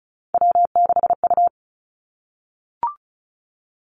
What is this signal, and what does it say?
Morse code, heard as a keyed tone, about 36 wpm, 700 Hz. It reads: W6U
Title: Three Challenging Letter-Number Combinations 35wpm